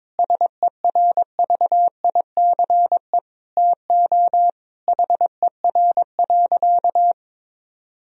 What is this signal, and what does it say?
Morse code, heard as a keyed tone, about 22 wpm, 700 Hz. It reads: SERVICE TO HER.
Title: Sentences from Top 1000 Words - Review Part 4 - 22wpm